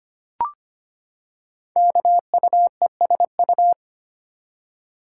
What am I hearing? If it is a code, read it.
Morse code, heard as a keyed tone, about 25 wpm, 700 Hz. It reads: KUESU
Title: Code Groups - 5 Letter:Numbers - 25wpm